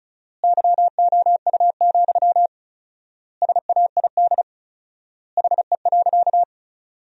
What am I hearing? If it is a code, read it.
Morse code, heard as a keyed tone, about 34 wpm, 700 Hz. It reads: YOU, SAID HE.
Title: 4x Word-Spacing - Sentences from Top 400 Words - 35wpm